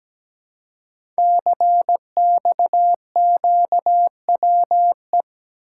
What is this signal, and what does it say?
Morse code, heard as a keyed tone, about 17 wpm, 700 Hz. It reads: CXQWE
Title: Code Groups - 5 Letters - 17wpm